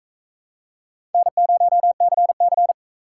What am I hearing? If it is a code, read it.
Morse code, heard as a keyed tone, about 40 wpm, 700 Hz. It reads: N0CC